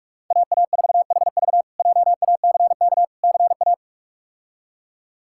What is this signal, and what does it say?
Morse code, heard as a keyed tone, about 42 wpm, 700 Hz. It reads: AA4HV JACK CA
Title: K1USN SST QSO Practice - 45wpm